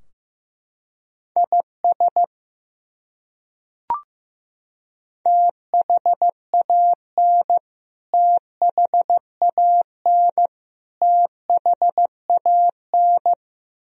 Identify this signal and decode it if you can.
Morse code, heard as a keyed tone, about 15 wpm, 700 Hz. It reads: IS THAN THAN THAN
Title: Top 100 Words Repeated 15wpm